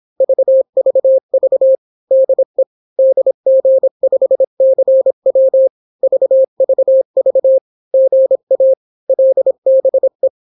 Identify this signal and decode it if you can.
Morse code, heard as a keyed tone, about 26 wpm, 530 Hz. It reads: VVV DE DG5CW VVV GA LBE